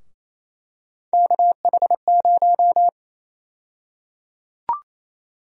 Morse code, heard as a keyed tone, about 28 wpm, 700 Hz. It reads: KH0
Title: USA Call Sign Prefixes - 28wpm